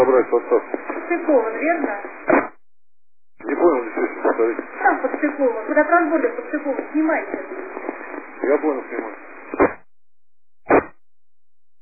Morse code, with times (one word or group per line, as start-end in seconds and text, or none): none